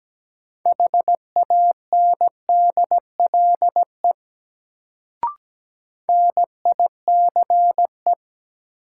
Morse code, heard as a keyed tone, about 17 wpm, 700 Hz. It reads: HANDLE NICE
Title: Sentences from Top 1100 Words - New Word Drill - 17wpm